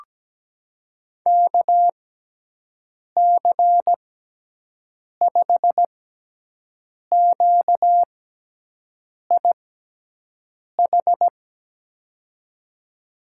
Morse code, heard as a keyed tone, about 17 wpm, spaced as for 6 wpm, 700 Hz. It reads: KC5QIH
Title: USA Call Signs - 6 Characters - 6x Character Spacing - 17wpm